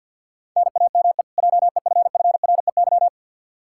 Morse code, heard as a keyed tone, about 45 wpm, 700 Hz. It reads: NAME JEFFREY
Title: Practice QSO - Exchange 1 - Beginning - 50wpm